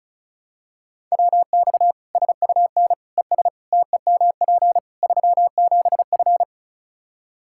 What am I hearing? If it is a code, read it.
Morse code, heard as a keyed tone, about 34 wpm, 700 Hz. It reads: WX SUN ES TEMP 37F
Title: K1USN SST QSO Practice 35wpm